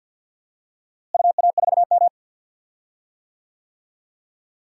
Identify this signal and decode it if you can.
Morse code, heard as a keyed tone, about 52 wpm, 700 Hz. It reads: UA4M
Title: DXCC-100 Call Signs - 4 Characters - Repeated 2x - 50wpm